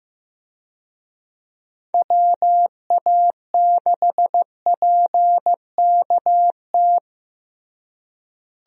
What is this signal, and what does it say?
Morse code, heard as a keyed tone, about 15 wpm, 700 Hz. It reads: WA6PKT
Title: USA Call Signs - 6 Characters - Repeated 2x - 15wpm